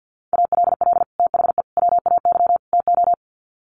Morse code, heard as a keyed tone, about 50 wpm, 700 Hz. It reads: ALL THE WAY TO
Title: Sets of 4 Words 50wpm